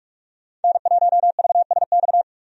Morse code, heard as a keyed tone, about 45 wpm, 700 Hz. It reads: N1VSX
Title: USA Call Signs - 5 Characters - 45wpm